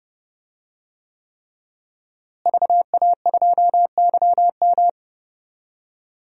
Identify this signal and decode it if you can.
Morse code, heard as a keyed tone, about 30 wpm, 700 Hz. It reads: VA2YM
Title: DXCC-100 Call Signs - 5 Characters - Repeated 3x - 30wpm